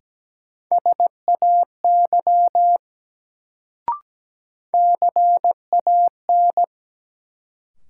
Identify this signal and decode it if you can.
Morse code, heard as a keyed tone, about 17 wpm, 700 Hz. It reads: SAY CAN
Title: Sets of 2 Words - New Word Drill - 17wpm